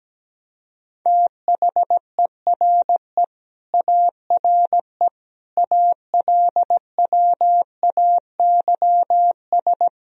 Morse code, heard as a keyed tone, about 17 wpm, 700 Hz. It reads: THERE ARE ALWAYS
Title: Sentences from Top 1000 Words - Review Part 4 - 17wpm